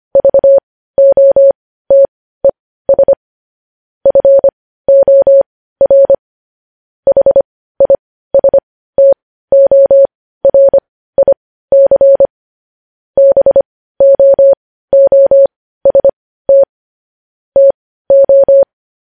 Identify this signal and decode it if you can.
Morse code, heard as a keyed tone, about 25 wpm, spaced as for 15 wpm, 550 Hz. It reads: VOTES FOR HISTORIC BOOST TO
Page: news-headlines-in-morse-code-at-15-wpm